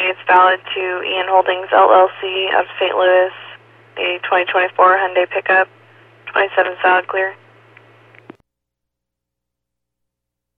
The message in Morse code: none

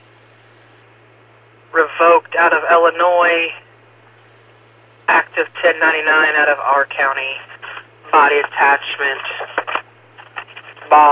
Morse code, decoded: none

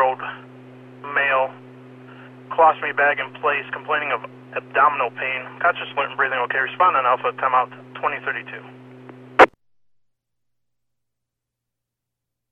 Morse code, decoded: none

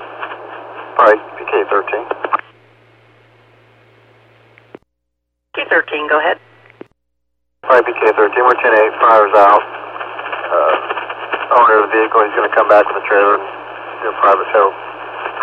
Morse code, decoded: none

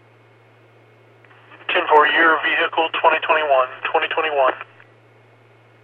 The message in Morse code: none

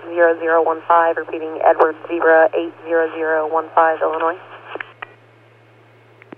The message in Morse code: none

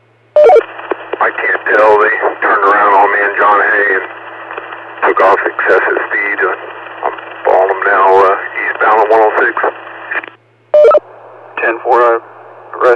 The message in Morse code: none

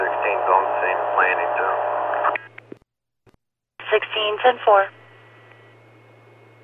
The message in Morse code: none

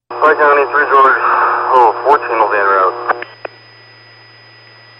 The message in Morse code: none

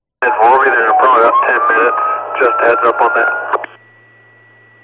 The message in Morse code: none

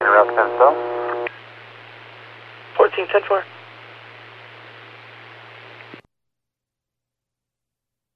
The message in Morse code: none

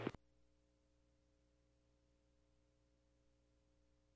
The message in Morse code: none

